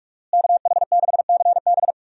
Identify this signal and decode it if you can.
Morse code, heard as a keyed tone, about 45 wpm, 700 Hz. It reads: KH6CB
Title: USA Call Signs - 5 Characters - Repeated 2x - 45wpm